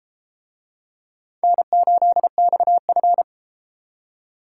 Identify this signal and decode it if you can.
Morse code, heard as a keyed tone, about 30 wpm, 700 Hz. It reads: N8XF